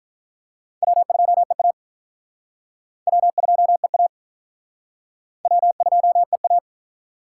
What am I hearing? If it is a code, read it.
Morse code, heard as a keyed tone, about 45 wpm, 700 Hz. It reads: W2EA W2EA W2EA